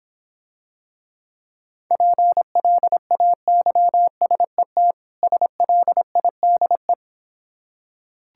Morse code, heard as a keyed tone, about 26 wpm, 700 Hz. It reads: PLAYSET SLIDE